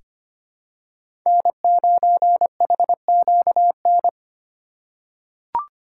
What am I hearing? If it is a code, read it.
Morse code, heard as a keyed tone, about 25 wpm, 700 Hz. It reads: N9HQN
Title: USA Call Signs - 5 Characters - 3x Character Spacing - 25wpm